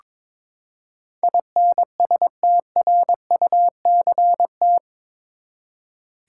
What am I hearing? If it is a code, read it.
Morse code, heard as a keyed tone, about 22 wpm, 700 Hz. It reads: INSTRUCT
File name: practice / 22wpm / Eight Letter Words - 22wpm